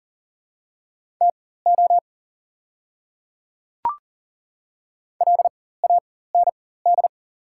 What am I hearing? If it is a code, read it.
Morse code, heard as a keyed tone, about 40 wpm, spaced as for 19 wpm, 700 Hz. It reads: TO LAND
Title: 4x Character Spacing - Top 200 Words - 40wpm